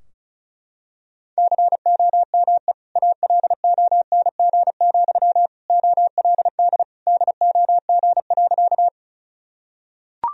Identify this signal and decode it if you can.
Morse code, heard as a keyed tone, about 36 wpm, 700 Hz. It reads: COME ALONG, OLD DOG.
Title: Sentences from Top 1000 Words - Review Part 2 - 35wpm